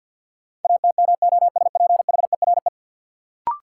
Morse code, heard as a keyed tone, about 49 wpm, 700 Hz. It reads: ATMOSPHERE
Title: Sentences from Top 1600 Words - New Word Drill Alt - 50wpm